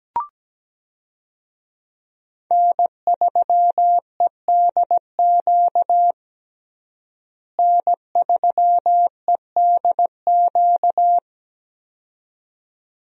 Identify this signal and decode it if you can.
Morse code, heard as a keyed tone, about 17 wpm, 700 Hz. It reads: N3EDQ N3EDQ
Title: USA Call Signs - 5 Characters - Repeated 2x - 17wpm